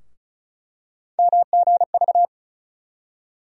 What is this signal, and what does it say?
Morse code, heard as a keyed tone, about 36 wpm, 700 Hz. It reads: MGV
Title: USA Call Sign Postfixes - 3 Letters 35wpm